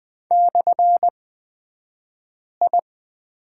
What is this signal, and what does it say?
Morse code, heard as a keyed tone, about 20 wpm, spaced as for 5 wpm, 700 Hz. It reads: /I
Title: 254 Lesson 41d - Period - Two Characters 20wpm